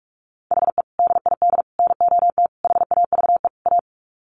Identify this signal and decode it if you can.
Morse code, heard as a keyed tone, about 44 wpm, 700 Hz. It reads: HE DID NOT HAVE A
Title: Sets of 5 Words Encore 45wpm